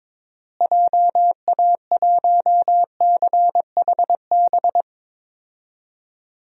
Morse code, heard as a keyed tone, about 22 wpm, 700 Hz. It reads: JA1CHB
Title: DXCC-100 Call Signs - 6 Characters - Repeated 3x - 22wpm